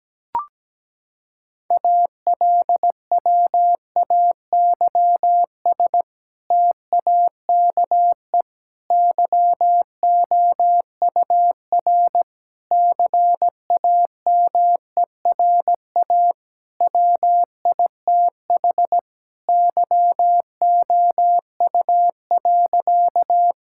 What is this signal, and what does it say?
Morse code, heard as a keyed tone, about 17 wpm, 700 Hz. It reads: ALWAYS TAKE YOUR CAMERA WITH YOU.
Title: Sentences from Top 1000 Words - Review Part 5 - 17wpm